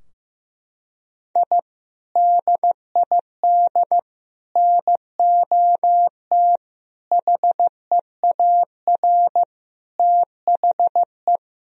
Morse code, heard as a keyed tone, about 15 wpm, 700 Hz. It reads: I DID NOT HEAR THE
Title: Sentences from Top 1000 Words - Review Part 2 - 15wpm